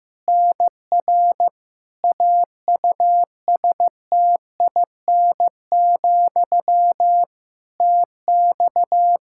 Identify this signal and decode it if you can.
Morse code, heard as a keyed tone, about 16 wpm, 700 Hz. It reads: NR AUSTIN, TX
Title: Practice QSO - Exchange 1 - Beginning - 15wpm